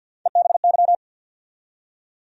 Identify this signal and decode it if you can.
Morse code, heard as a keyed tone, about 49 wpm, 700 Hz. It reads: EBY
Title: USA Call Sign Postfixes - 3 Letters 50wpm